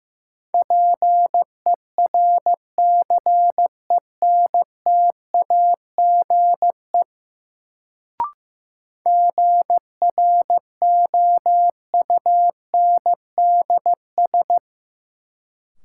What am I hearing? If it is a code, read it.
Morse code, heard as a keyed tone, about 15 wpm, 700 Hz. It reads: PERCENTAGE GROUNDS
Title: Sentences from Top 1300 Words - New Word Drill - 15wpm